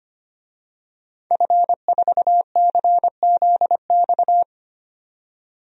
Morse code, heard as a keyed tone, about 25 wpm, 700 Hz. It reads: F4CZX
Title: DXCC-100 Call Signs - 5 Characters - 25wpm